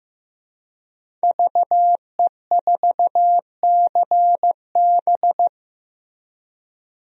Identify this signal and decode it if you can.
Morse code, heard as a keyed tone, about 16 wpm, 700 Hz. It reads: VE4CB
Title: DXCC-100 Call Signs - 5 Characters - Repeated 3x - 15wpm